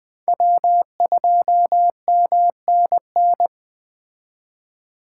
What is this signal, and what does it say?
Morse code, heard as a keyed tone, about 20 wpm, 700 Hz. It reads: W2MNN
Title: Speed Racing - USA Call Signs 17wpm